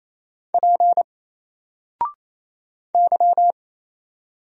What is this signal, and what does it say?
Morse code, heard as a keyed tone, about 28 wpm, 700 Hz. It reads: P Y
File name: Single-Letter-Number-28wpm